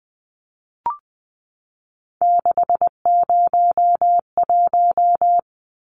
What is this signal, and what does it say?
Morse code, heard as a keyed tone, about 20 wpm, 700 Hz. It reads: 601
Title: Numbers - 3-Digit 20wpm